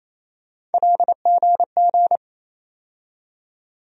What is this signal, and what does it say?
Morse code, heard as a keyed tone, about 28 wpm, 700 Hz. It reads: LGG
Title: USA Call Sign Postfixes - 3 Letters 28wpm